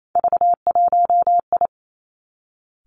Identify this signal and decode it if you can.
Morse code, heard as a keyed tone, about 28 wpm, 700 Hz. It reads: V1I
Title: Three Challenging Letter-Number Combinations 28wpm